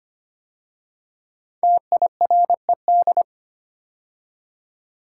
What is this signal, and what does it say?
Morse code, heard as a keyed tone, about 25 wpm, 700 Hz. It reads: TIRED
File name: 5 Letter Words - First 3 Letters Given - 25wpm